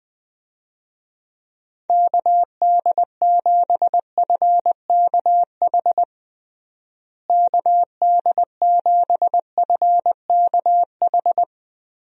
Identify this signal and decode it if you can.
Morse code, heard as a keyed tone, about 20 wpm, 700 Hz. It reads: KD7FKH KD7FKH
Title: USA Call Signs - 6 Characters - Repeated 2x - 20wpm